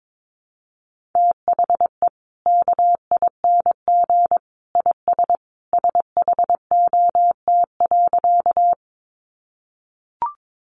Morse code, heard as a keyed tone, about 22 wpm, 700 Hz. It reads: THE KING IS SHOT.
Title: Sentences from Top 1100 Words 22wpm